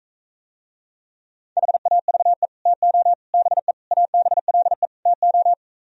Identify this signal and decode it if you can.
Morse code, heard as a keyed tone, about 43 wpm, 700 Hz. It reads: HAVE TO BE ABLE TO